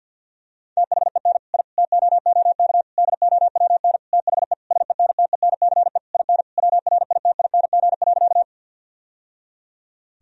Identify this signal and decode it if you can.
Morse code, heard as a keyed tone, about 52 wpm, 700 Hz. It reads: THEN I TOOK DOWN THE SENTENCE IN WRITING.
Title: Sentences from Top 1000 Words - Review Part 1 - 50wpm